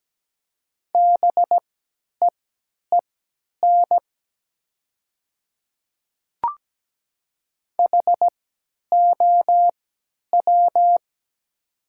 Morse code, heard as a keyed tone, about 17 wpm, spaced as for 10 wpm, 700 Hz. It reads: BEEN HOW
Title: 3x Character Spacing - Top 300 Words - 17wpm